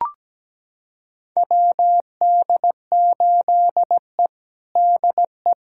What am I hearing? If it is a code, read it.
Morse code, heard as a keyed tone, about 17 wpm, 700 Hz. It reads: WD8E DE
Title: USA Callsign DE Callsign - 4 Characters - 17wpm